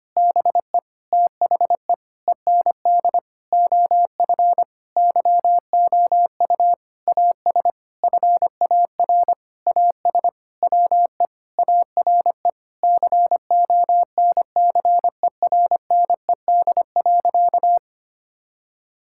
Morse code, heard as a keyed tone, about 25 wpm, 700 Hz. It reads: BE THE END OF YOU AS FAR AS WE ARE CONCERNED.
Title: Sentences from Top 1000 Words - Review Part 6 - 25wpm